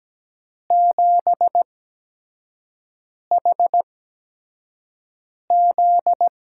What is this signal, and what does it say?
Morse code, heard as a keyed tone, about 17 wpm, spaced as for 5 wpm, 700 Hz. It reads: 7HZ
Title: USA Call Signs - 5 Characters - 8x Character Spacing - 17wpm